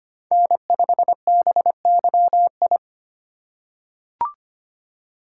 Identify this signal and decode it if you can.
Morse code, heard as a keyed tone, about 25 wpm, 700 Hz. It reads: N5BYI
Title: USA Call Signs - 5 Characters - 2x Character Spacing - 25wpm